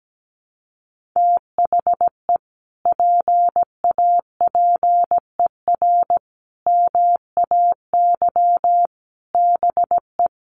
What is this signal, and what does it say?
Morse code, heard as a keyed tone, about 17 wpm, 700 Hz. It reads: THE PAPER MAY BE
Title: Sentences from Top 1400 Words 17wpm